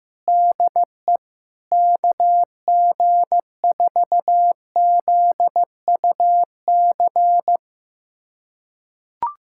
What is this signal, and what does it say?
Morse code, heard as a keyed tone, about 15 wpm, 700 Hz. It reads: DE KG4ZUC
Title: USA Callsign DE Callsign - 6 Characters - 15wpm